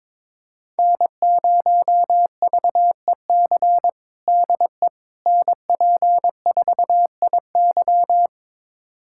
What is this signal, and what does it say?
Morse code, heard as a keyed tone, about 22 wpm, 700 Hz. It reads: N0VEC DE NP4IY
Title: USA Callsign DE Callsign - 5 Characters - 22wpm